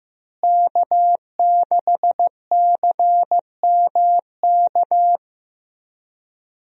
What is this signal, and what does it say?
Morse code, heard as a keyed tone, about 15 wpm, 700 Hz. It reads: K6CMK
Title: USA Call Signs - 5 Characters - Repeated 3x - 15wpm